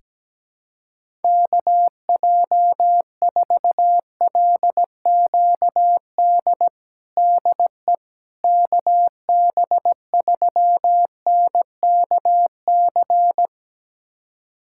Morse code, heard as a keyed tone, about 17 wpm, 700 Hz. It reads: KJ4LQD DE KB3NKC
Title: USA Callsign DE Callsign - 6 Characters - 17wpm